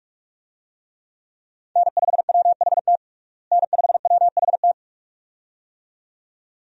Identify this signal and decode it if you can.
Morse code, heard as a keyed tone, about 45 wpm, 700 Hz. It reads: N5WHT N5WHT
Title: USA Call Signs - 5 Characters - Repeated 2x - 45wpm